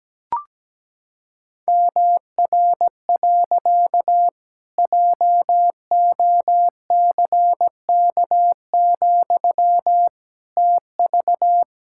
Morse code, heard as a keyed tone, about 17 wpm, 700 Hz. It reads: MR. JOCK, TV
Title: Pangram Challenge v2 - 17wpm